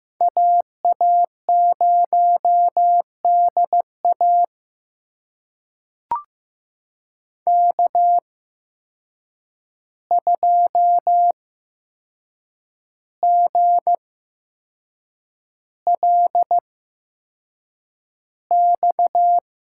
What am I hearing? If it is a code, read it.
Morse code, heard as a keyed tone, about 15 wpm, spaced as for 6 wpm, 700 Hz. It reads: AA0DA K2GLX
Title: USA Call Signs - 5 Characters - 8x Character Spacing - 15wpm